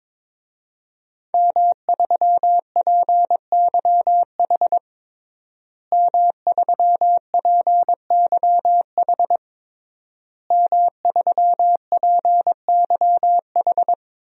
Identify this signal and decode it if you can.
Morse code, heard as a keyed tone, about 22 wpm, 700 Hz. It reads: M3PYH M3PYH M3PYH
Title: DXCC-100 Call Signs - 5 Characters - Repeated 3x - 22wpm